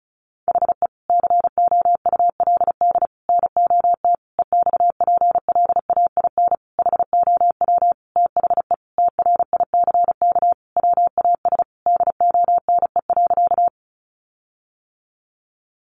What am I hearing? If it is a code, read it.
Morse code, heard as a keyed tone, about 34 wpm, 700 Hz. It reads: HE COULD NOT EXPLAIN HOW THE TRICK WAS DONE.
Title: Sentences from Top 1500 Words 35wpm